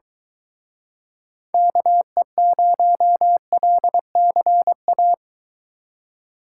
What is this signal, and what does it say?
Morse code, heard as a keyed tone, about 24 wpm, 700 Hz. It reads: KE0LCA